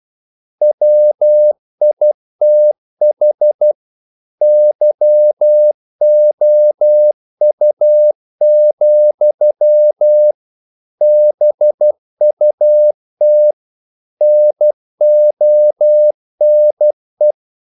Morse code, heard as a keyed tone, about 12 wpm, 600 Hz. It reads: WITH YOU, BUT NONE
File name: CW / Practice / If12